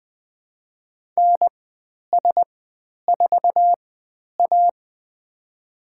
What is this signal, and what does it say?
Morse code, heard as a keyed tone, about 20 wpm, spaced as for 10 wpm, 700 Hz. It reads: NS4A
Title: 095 Lesson 16g - Letter c - Call Signs 20wpm